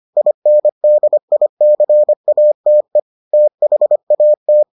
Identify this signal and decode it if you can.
Morse code, heard as a keyed tone, about 25 wpm, 600 Hz. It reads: INDICATE THAT